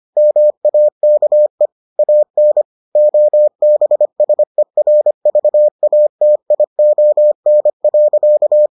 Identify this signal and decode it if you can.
Morse code, heard as a keyed tone, about 25 wpm, 600 Hz. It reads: MAKE AN OBSERVATION.